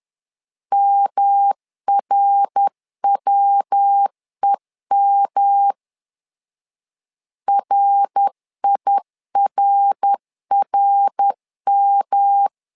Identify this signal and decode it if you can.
Morse code, heard as a keyed tone, about 10 wpm, 800 Hz. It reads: MRWEM RIRRM
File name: LEZIONI / 3 MWRO / 3 MWR 10wpm 1952